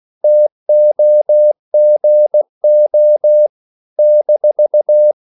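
Morse code, heard as a keyed tone, about 16 wpm, 600 Hz. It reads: TOGO -
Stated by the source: CW practice of national capitals at 16 words per minute, from ON6ZQ.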